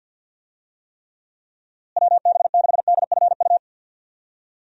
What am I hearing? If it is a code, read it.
Morse code, heard as a keyed tone, about 52 wpm, 700 Hz. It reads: WB6DFU
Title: USA Call Signs - 6 Characters - Repeated 2x - 50wpm